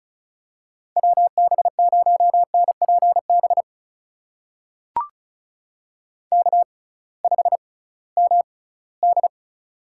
Morse code, heard as a keyed tone, about 36 wpm, spaced as for 20 wpm, 700 Hz. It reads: WB0NPB K5MD
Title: USA Call Signs - 6x Character Spacing - 35wpm